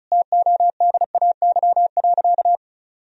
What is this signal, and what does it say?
Morse code, heard as a keyed tone, about 36 wpm, 700 Hz. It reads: TODAY.
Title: Sentences from Top 1000 Words - Review Part 4 - 35wpm